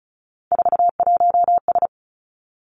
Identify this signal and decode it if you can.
Morse code, heard as a keyed tone, about 36 wpm, 700 Hz. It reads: 41S
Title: Three Challenging Letter-Number Combinations 35wpm